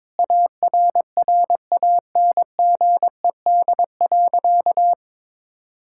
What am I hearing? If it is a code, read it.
Morse code, heard as a keyed tone, about 22 wpm, 700 Hz. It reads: ARRANGED.
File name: practice / 22wpm / Sentences with 3-Word Phrases - 22wpm